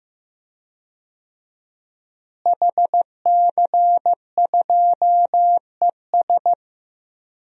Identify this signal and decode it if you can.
Morse code, heard as a keyed tone, about 15 wpm, 700 Hz. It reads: HC2ES
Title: DXCC-100 Call Signs - 5 Characters - Repeated 3x - 15wpm